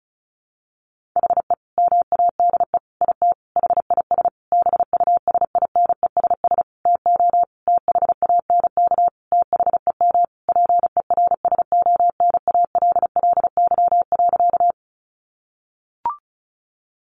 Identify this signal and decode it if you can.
Morse code, heard as a keyed tone, about 34 wpm, 700 Hz. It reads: HE MADE IT HIS BUSINESS TO THANK THEM PERSONALLY.
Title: Sentences from Top 1400 Words 35wpm